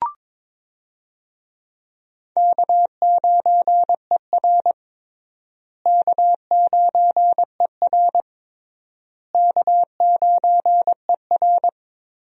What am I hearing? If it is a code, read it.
Morse code, heard as a keyed tone, about 22 wpm, 700 Hz. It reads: K9ER K9ER K9ER
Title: USA Call Signs - 4 Characters - Repeated 3x - 22wpm